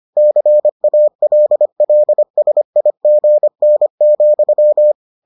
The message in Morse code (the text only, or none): CALLSIGN,